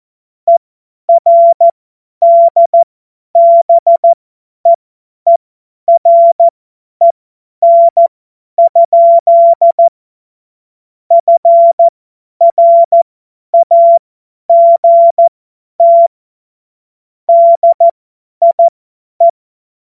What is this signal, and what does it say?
Morse code, heard as a keyed tone, about 14 wpm, spaced as for 10 wpm, 680 Hz. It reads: ERDBEEREN? FRAGT DIE
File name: Morsekurs / Witze / Witz 25. Januar 2022, 50 F70